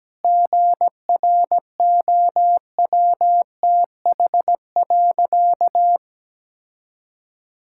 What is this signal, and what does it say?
Morse code, heard as a keyed tone, about 17 wpm, 700 Hz. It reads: GROWTH.
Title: Sentences from Top 1000 Words - Review Part 5 - 17wpm